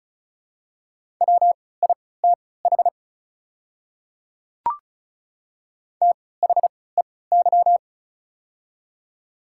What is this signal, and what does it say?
Morse code, heard as a keyed tone, about 36 wpm, spaced as for 20 wpm, 700 Hz. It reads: WITH THEY
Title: 3x Character Spacing - Top 200 Words - 35wpm